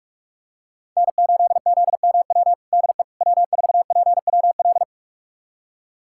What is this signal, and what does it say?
Morse code, heard as a keyed tone, about 46 wpm, 700 Hz. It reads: N8ZMW DE W4PWL
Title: USA Callsign DE Callsign - 5 Characters - 45wpm